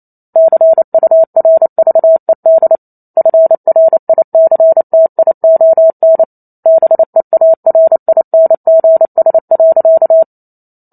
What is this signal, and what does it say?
Morse code, heard as a keyed tone, about 28 wpm, 650 Hz. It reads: CURVED FRICTION BEARINGS.
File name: Chapter 2 - 28 WPM